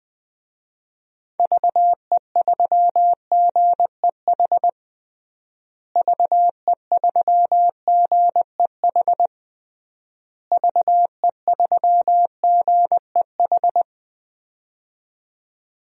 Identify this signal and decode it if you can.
Morse code, heard as a keyed tone, about 20 wpm, 700 Hz. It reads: VE3GEH VE3GEH VE3GEH
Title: DXCC-100 Call Signs - 6 Characters - Repeated 3x - 20wpm